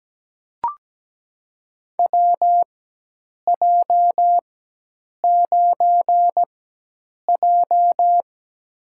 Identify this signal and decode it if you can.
Morse code, heard as a keyed tone, about 17 wpm, spaced as for 8 wpm, 700 Hz. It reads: WJ9J
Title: USA Call Signs - 4 Characters - 4x Character Spacing - 17wpm